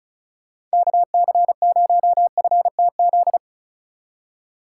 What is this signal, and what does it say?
Morse code, heard as a keyed tone, about 34 wpm, 700 Hz. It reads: KC0FTZ
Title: USA Call Signs - 6 Characters - 2x Character Spacing - 35wpm